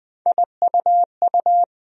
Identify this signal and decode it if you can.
Morse code, heard as a keyed tone, about 21 wpm, 700 Hz. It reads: IUU
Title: USA Call Sign Postfixes - 3 Letters 20wpm